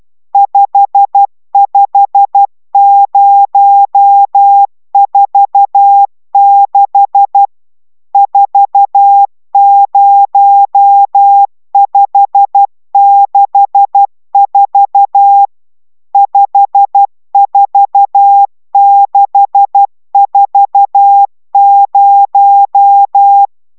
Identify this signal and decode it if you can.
Morse code, heard as a keyed tone, about 12 wpm, 800 Hz. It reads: 55046 40564 54640